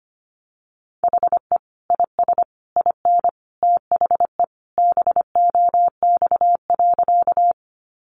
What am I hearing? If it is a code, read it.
Morse code, heard as a keyed tone, about 25 wpm, 700 Hz. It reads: HE IS IN THE BOX.
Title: Sentences from Top 500 Words 25wpm